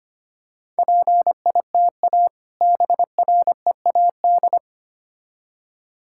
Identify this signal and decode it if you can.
Morse code, heard as a keyed tone, about 25 wpm, 700 Hz. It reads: PITA BREAD
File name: practice / 25wpm / Difficult Words with Hints - 25wpm